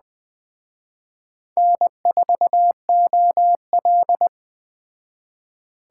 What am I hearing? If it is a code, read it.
Morse code, heard as a keyed tone, about 20 wpm, 700 Hz. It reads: N4OL
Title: Speed Racing - USA 1x2 Call Signs 17wpm